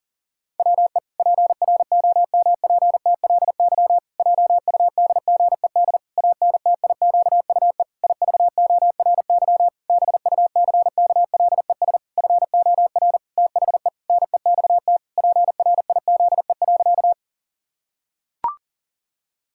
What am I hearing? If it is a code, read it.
Morse code, heard as a keyed tone, about 37 wpm, 700 Hz. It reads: WE PROMPTLY JUDGED ANTIQUE IVORY BUCKLES FOR THE NEXT PRIZE.
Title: Pangram Challenge v2 - 40wpm